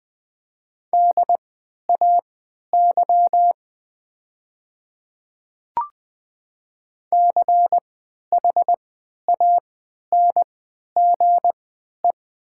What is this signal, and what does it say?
Morse code, heard as a keyed tone, about 20 wpm, spaced as for 11 wpm, 700 Hz. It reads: DAY CHANGE
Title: 3x Character Spacing - Top 200 Words - 20wpm